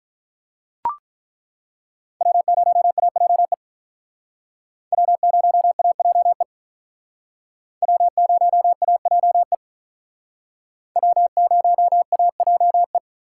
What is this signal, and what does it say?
Morse code, heard as a keyed tone, about 42 wpm, 700 Hz. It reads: W0AJE W0AJE W0AJE W0AJE
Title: Speed Racing - USA 1x3 Call Signs 35wpm